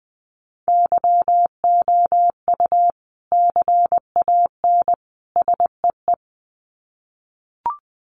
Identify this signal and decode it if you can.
Morse code, heard as a keyed tone, about 20 wpm, 700 Hz. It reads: YOU CAN SEE
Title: Sets of 3 Words 20wpm